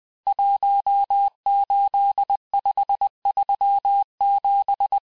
Morse code, heard as a keyed tone, about 20 wpm, 800 Hz. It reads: 18537
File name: cours-858-texte-chiffre-vitesse-19mpm